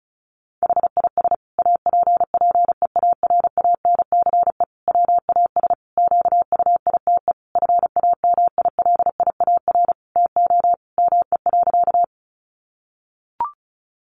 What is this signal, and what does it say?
Morse code, heard as a keyed tone, about 34 wpm, 700 Hz. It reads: HIS APPEARANCE WAS QUITE FAMILIAR TO ME.
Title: Sentences from Top 1100 Words 35wpm